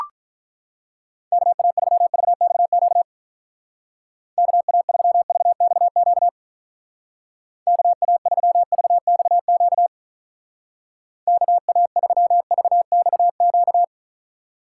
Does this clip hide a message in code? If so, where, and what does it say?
Yes: Morse code from 1.3 to 13.8 s, heard as a keyed tone, about 44 wpm, 700 Hz. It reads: KA3VXQ KA3VXQ KA3VXQ KA3VXQ